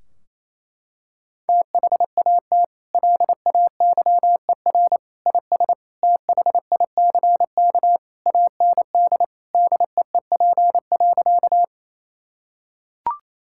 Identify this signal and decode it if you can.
Morse code, heard as a keyed tone, about 28 wpm, 700 Hz. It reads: THAT LAYER IS THICK AND DEEP.
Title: Sentences from Top 1600 Words 28wpm